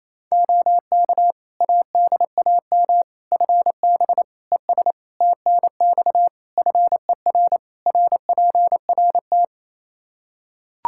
Morse code, heard as a keyed tone, about 27 wpm, 700 Hz. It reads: OK ADAM FB ES TNX FER RPRT
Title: Practice QSO - Exchange 2 - Info - 28wpm